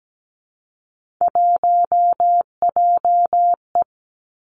Morse code, heard as a keyed tone, about 17 wpm, 700 Hz. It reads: 1JE
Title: Three Challenging Letter-Number Combinations 17wpm